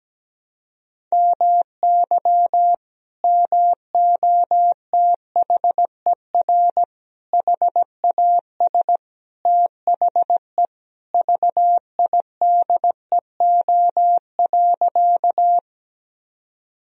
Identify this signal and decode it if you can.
Morse code, heard as a keyed tone, about 17 wpm, 700 Hz. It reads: MY MOTHER HAS THE VIDEO.
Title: Sentences from Top 1000 Words - Review Part 4 - 17wpm